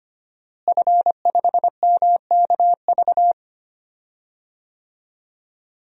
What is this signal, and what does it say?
Morse code, heard as a keyed tone, about 25 wpm, 700 Hz. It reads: F5MKV
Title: DXCC-100 Call Signs - 5 Characters - Repeated 3x - 25wpm